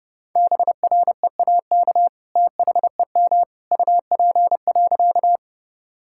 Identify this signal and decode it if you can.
Morse code, heard as a keyed tone, about 28 wpm, 700 Hz. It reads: BREAK THEM UP.
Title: Sentences from Top 1000 Words - Review Part 6 - 30wpm